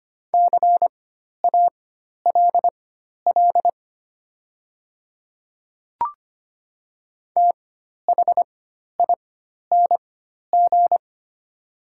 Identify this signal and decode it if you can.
Morse code, heard as a keyed tone, about 25 wpm, spaced as for 12 wpm, 700 Hz. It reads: CALL THING
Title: 4x Character Spacing - Top 200 Words - 25wpm